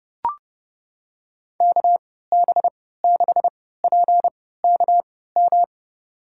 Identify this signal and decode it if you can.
Morse code, heard as a keyed tone, about 30 wpm, spaced as for 17 wpm, 700 Hz. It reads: KB6PKM
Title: USA Call Signs - 6 Characters - 3x Character Spacing - 30wpm